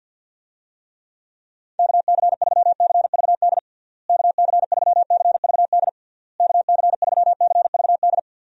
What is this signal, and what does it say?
Morse code, heard as a keyed tone, about 47 wpm, 700 Hz. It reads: KC3CVD KC3CVD KC3CVD
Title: USA Call Signs - 6 Characters - Repeated 3x - 50wpm